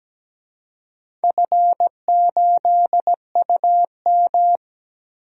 Morse code, heard as a keyed tone, about 17 wpm, 700 Hz. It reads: F8UM
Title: DXCC-100 Call Signs - 4 Characters - Repeated 2x - 17wpm